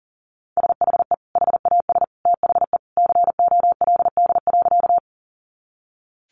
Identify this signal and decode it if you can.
Morse code, heard as a keyed tone, about 40 wpm, 700 Hz. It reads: SHE HAS THE COLD.
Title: Easy Phrases 40wpm